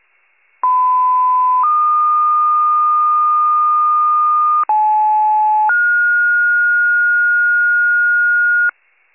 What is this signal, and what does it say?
Two-tone page, heard as a one-second tone, then a three-second tone, about 1010 then 1220 Hz, about 860 then 1430 Hz, with noise. It is heard from 0.6 to 8.7 s.
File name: pager / 94 RESCUE OFC / wcalert